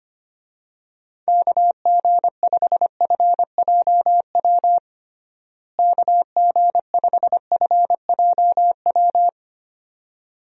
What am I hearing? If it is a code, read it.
Morse code, heard as a keyed tone, about 25 wpm, 700 Hz. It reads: KG5FJW KG5FJW
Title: USA Call Signs - 6 Characters - Repeated 2x - 25wpm